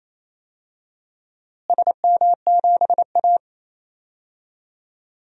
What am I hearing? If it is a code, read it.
Morse code, heard as a keyed tone, about 28 wpm, 700 Hz. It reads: SM7A
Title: DXCC-100 Call Signs - 4 Characters - Repeated 3x - 28wpm